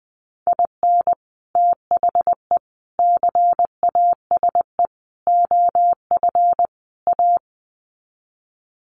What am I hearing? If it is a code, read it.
Morse code, heard as a keyed tone, about 20 wpm, 700 Hz. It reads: IN THE CASE OF A
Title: Sets of 5 Words Encore 20wpm